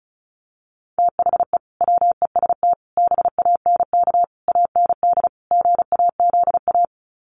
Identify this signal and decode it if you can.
Morse code, heard as a keyed tone, about 34 wpm, 700 Hz. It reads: THE WEST BANK AND GAZA
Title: Sets of 5 Words 35wpm